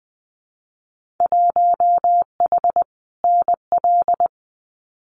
Morse code, heard as a keyed tone, about 20 wpm, 700 Hz. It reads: 1H NL
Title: Winter Field Day Exchanges 20wpm